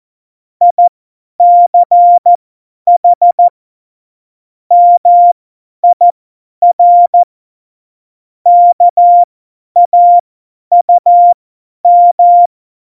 Morse code, heard as a keyed tone, about 14 wpm, spaced as for 10 wpm, 700 Hz. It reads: ICH MIR KAUM
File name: Morsekurs / Witze / Witz 10. Oktober 17, Tempo 50